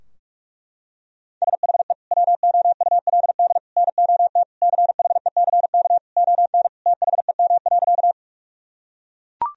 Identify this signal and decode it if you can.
Morse code, heard as a keyed tone, about 41 wpm, 700 Hz. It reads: SHE WOULD NOT CHECK ON THEM.
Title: Sentences from Top 1000 Words - Review Part 3 - 45wpm